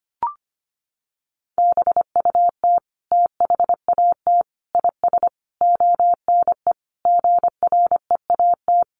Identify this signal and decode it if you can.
Morse code, heard as a keyed tone, about 25 wpm, 700 Hz. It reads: BUT THAT IS ONE GREAT
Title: Sentences from Top 800 Words 25wpm